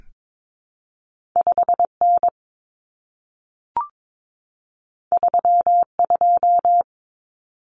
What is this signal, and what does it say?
Morse code, heard as a keyed tone, about 22 wpm, 700 Hz. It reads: 5N 32
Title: Common Cut Numbers - 2-Digit 22wpm